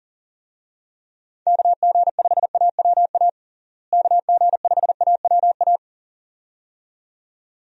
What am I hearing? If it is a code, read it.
Morse code, heard as a keyed tone, about 40 wpm, 700 Hz. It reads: KG5AWA KG5AWA
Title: USA Call Signs - 6 Characters - Repeated 2x - 40wpm